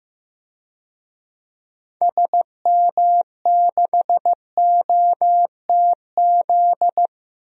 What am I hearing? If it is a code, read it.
Morse code, heard as a keyed tone, about 15 wpm, 700 Hz. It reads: SM6OTZ
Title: DXCC-100 Call Signs - 6 Characters - Repeated 2x - 15wpm